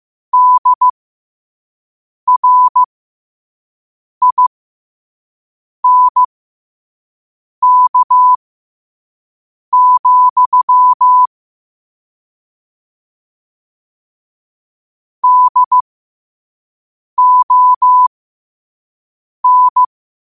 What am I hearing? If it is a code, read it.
Morse code, heard as a keyed tone, about 15 wpm, spaced as for 5 wpm, 1000 Hz. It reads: DRINK, DON
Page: New quotes every day in morse code at 5 Words per minute.